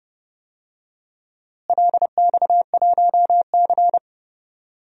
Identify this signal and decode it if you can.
Morse code, heard as a keyed tone, about 30 wpm, 700 Hz. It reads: LX1C
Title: DXCC-100 Call Signs - 4 Characters - Repeated 2x - 30wpm